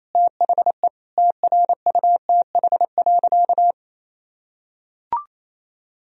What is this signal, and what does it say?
Morse code, heard as a keyed tone, about 27 wpm, 690 Hz. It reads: THE TRUTH.
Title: Sentences from Top 1000 Words - Review Part 3 - 28wpm